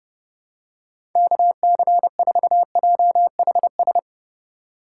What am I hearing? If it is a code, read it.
Morse code, heard as a keyed tone, about 31 wpm, 700 Hz. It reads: KC4JHS
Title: USA Call Signs - 6 Characters - 6x Character Spacing - 30wpm